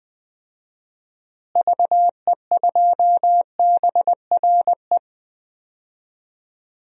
Morse code, heard as a keyed tone, about 20 wpm, 700 Hz. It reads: VE2BRE
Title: DXCC-100 Call Signs - 6 Characters - Repeated 3x - 20wpm